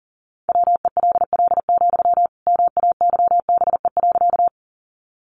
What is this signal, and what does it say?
Morse code, heard as a keyed tone, about 40 wpm, 700 Hz. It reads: WELL, MAYBE.
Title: Sentences from Top 1000 Words 40wpm